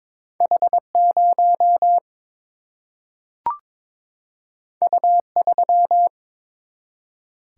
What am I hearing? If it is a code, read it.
Morse code, heard as a keyed tone, about 22 wpm, 700 Hz. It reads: H0 U3
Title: Single-Letter Single-Number 22wpm